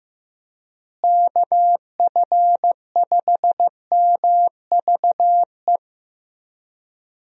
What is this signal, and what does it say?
Morse code, heard as a keyed tone, about 15 wpm, 700 Hz. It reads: KF5MVE
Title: USA Call Signs - 6 Characters - Repeated 3x - 15wpm